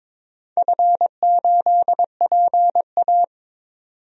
Morse code, heard as a keyed tone, about 22 wpm, 700 Hz. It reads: F8PA
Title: DXCC-100 Call Signs - 4 Characters - 22wpm